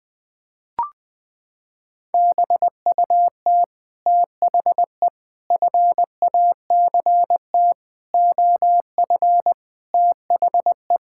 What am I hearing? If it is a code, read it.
Morse code, heard as a keyed tone, about 20 wpm, 700 Hz. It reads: BUT THE FACT OF THE
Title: Sets of 8 Words - 20wpm